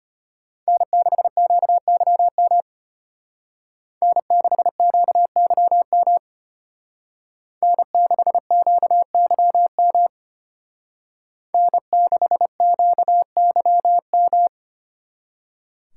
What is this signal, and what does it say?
Morse code, heard as a keyed tone, about 30 wpm, 700 Hz. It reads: N6QYM N6QYM N6QYM N6QYM
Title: Speed Racing - USA Call Signs 25wpm